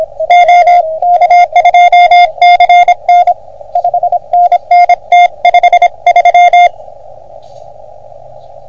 Morse code, heard as a keyed tone, about 26 wpm, 720 Hz, with radio noise. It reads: OK2CN 5NN T53
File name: OM3PA PILUP CUC 388